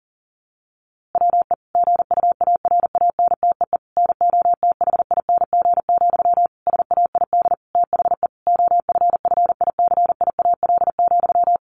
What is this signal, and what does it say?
Morse code, heard as a keyed tone, about 40 wpm, 700 Hz. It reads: WE GUARANTEE NOTHING, SAID THE OFFICIAL,
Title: A Sound Of Thunder 40wpm